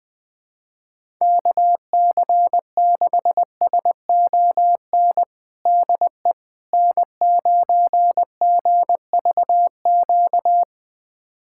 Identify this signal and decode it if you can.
Morse code, heard as a keyed tone, about 20 wpm, 700 Hz. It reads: KC6SON DE N9GVQ
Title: USA Callsign DE Callsign 20wpm